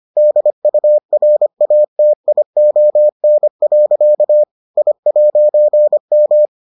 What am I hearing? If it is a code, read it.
Morse code, heard as a keyed tone, about 25 wpm, 600 Hz. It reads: DURATION. I'M